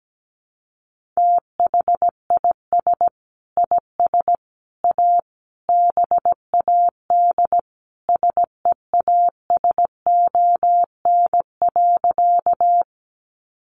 Morse code, heard as a keyed tone, about 17 wpm, 700 Hz. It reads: THIS IS A BAD SEASON.
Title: Sentences from Top 1400 Words 17wpm